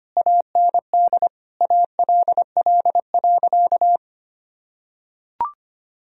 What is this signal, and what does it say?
Morse code, heard as a keyed tone, about 25 wpm, 700 Hz. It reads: AND ALL.
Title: Sentences from Top 1000 Words - Review Part 2 - 25wpm